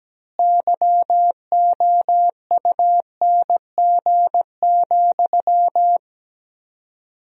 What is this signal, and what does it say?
Morse code, heard as a keyed tone, about 17 wpm, 700 Hz. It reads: YOUNG,
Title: 4x Word-Spacing - Sentences from Top 400 Words - 17wpm